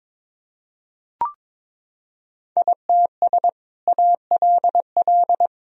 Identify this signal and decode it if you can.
Morse code, heard as a keyed tone, about 22 wpm, 700 Hz. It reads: ITS ALL
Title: Sentences from Top 1000 Words - Review Part 2 - 22wpm